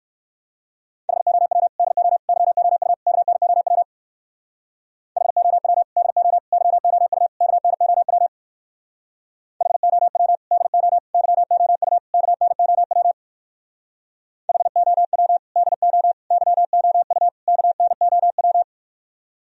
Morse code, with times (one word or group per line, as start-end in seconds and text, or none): 1.1-1.7 HOW
1.8-2.2 DO
2.3-2.9 YOU
3.1-3.8 KNOW
5.2-5.8 HOW
6.0-6.4 DO
6.5-7.3 YOU
7.4-8.3 KNOW
9.6-10.4 HOW
10.5-11.0 DO
11.1-12.0 YOU
12.1-13.1 KNOW
14.5-15.4 HOW
15.6-16.1 DO
16.3-17.3 YOU
17.5-18.6 KNOW